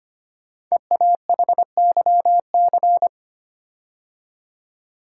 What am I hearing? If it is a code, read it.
Morse code, heard as a keyed tone, about 25 wpm, 700 Hz. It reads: EAHYC
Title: Code Groups - 5 Letters - 25wpm